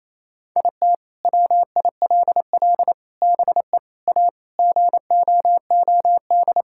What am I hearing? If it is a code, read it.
Morse code, heard as a keyed tone, about 28 wpm, 700 Hz. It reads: IT WILL BE A GOOD
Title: Sentences from Top 1000 Words - Review Part 6 - 28wpm